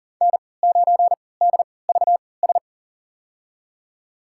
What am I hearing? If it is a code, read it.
Morse code, heard as a keyed tone, about 40 wpm, spaced as for 23 wpm, 700 Hz. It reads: N9DVS
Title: USA Call Signs - 5 Characters - 3x Character Spacing - 40wpm